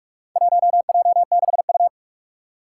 Morse code, heard as a keyed tone, about 44 wpm, 700 Hz. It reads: 1J6U
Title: Four Challenging Letter-Number Combinations 45wpm